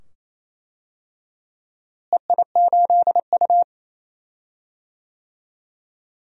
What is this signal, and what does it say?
Morse code, heard as a keyed tone, about 27 wpm, 700 Hz. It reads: EI8U
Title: DXCC-100 Call Signs - 4 Characters - Repeated 2x - 28wpm